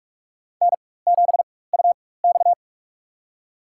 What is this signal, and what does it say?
Morse code, heard as a keyed tone, about 46 wpm, spaced as for 21 wpm, 700 Hz. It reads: N7UX
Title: USA Call Signs - 4 Characters - 4x Character Spacing - 45wpm